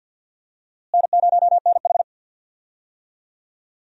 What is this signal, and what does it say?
Morse code, heard as a keyed tone, about 52 wpm, 700 Hz. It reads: N0NH
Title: USA Call Signs - 4 Characters - Repeated 2x - 50wpm